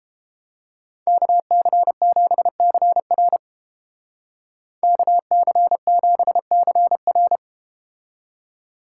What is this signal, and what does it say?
Morse code, heard as a keyed tone, about 32 wpm, 700 Hz. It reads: KC7CR KC7CR